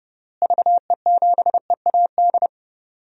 Morse code, heard as a keyed tone, about 30 wpm, 700 Hz. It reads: VE7EAD
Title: DXCC-100 Call Signs - 6 Characters - Repeated 3x - 30wpm